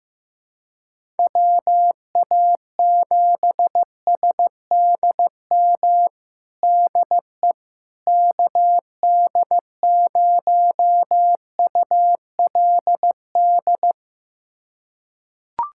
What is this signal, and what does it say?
Morse code, heard as a keyed tone, about 15 wpm, 700 Hz. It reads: WA7SDM DE KD0ULD
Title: USA Callsign DE Callsign - 6 Characters - 15wpm